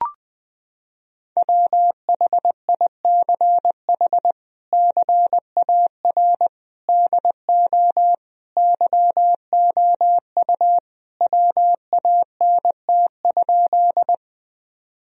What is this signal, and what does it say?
Morse code, heard as a keyed tone, about 21 wpm, 700 Hz. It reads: WHICH CAR DO YOU WANT?
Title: Sentences from Top 1000 Words - Review Part 1 - 20wpm